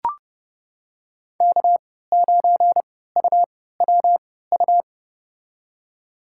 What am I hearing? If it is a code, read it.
Morse code, heard as a keyed tone, about 32 wpm, spaced as for 17 wpm, 700 Hz. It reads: K9UWU
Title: USA Call Signs - 5 Characters - 3x Character Spacing - 30wpm